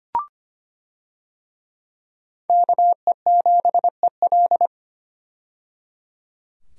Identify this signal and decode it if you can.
Morse code, heard as a keyed tone, about 25 wpm, 700 Hz. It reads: KE7EL
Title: USA Call Signs - 5 Characters - 25wpm